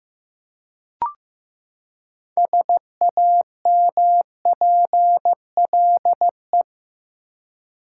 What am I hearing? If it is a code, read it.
Morse code, heard as a keyed tone, about 15 wpm, 700 Hz. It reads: SAMPLE
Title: Six Letter Words - 15wpm